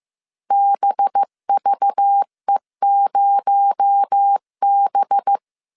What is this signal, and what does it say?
Morse code, heard as a keyed tone, about 15 wpm, 800 Hz. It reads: BVE0B